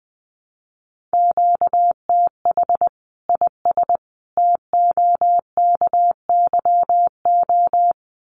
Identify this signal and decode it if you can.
Morse code, heard as a keyed tone, about 20 wpm, 700 Hz. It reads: QTH IS TOKYO
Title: Common QSO Phrases 20wpm